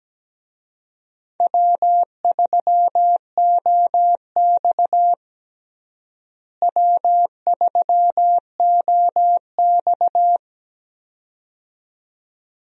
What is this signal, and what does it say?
Morse code, heard as a keyed tone, about 17 wpm, 700 Hz. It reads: W3OX W3OX
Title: USA Call Signs - 4 Characters - Repeated 2x - 17wpm